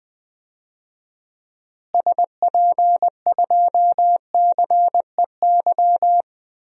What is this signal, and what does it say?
Morse code, heard as a keyed tone, about 21 wpm, 700 Hz. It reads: SP2CEY
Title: DXCC-100 Call Signs - 6 Characters - Repeated 3x - 20wpm